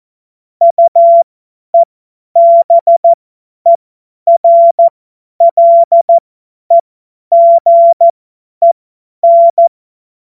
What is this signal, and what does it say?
Morse code, heard as a keyed tone, about 14 wpm, spaced as for 10 wpm, 680 Hz. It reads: UEBERLEGEN
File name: Morsekurs / Witze / Witz 21. Juni 2022 50 F70